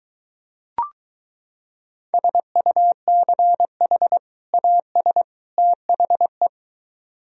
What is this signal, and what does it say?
Morse code, heard as a keyed tone, about 23 wpm, 700 Hz. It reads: SUCH AS THE